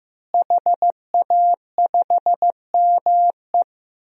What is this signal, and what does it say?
Morse code, heard as a keyed tone, about 15 wpm, 700 Hz. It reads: HA5ME
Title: DXCC-100 Call Signs - 5 Characters - Repeated 3x - 15wpm